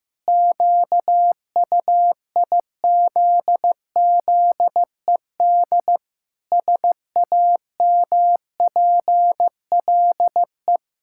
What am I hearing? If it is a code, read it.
Morse code, heard as a keyed tone, about 15 wpm, 700 Hz. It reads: QUIZZED SAMPLE
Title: Pangram Challenge v2 - 15wpm